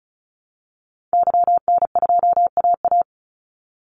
Morse code, heard as a keyed tone, about 34 wpm, 700 Hz. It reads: YN2AA
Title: Call Signs Repeated 3x - 35wpm